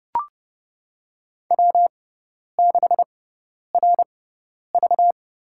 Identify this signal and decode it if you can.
Morse code, heard as a keyed tone, about 30 wpm, spaced as for 10 wpm, 700 Hz. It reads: W6RV
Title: USA Call Signs - 4 Characters - 6x Character Spacing - 30wpm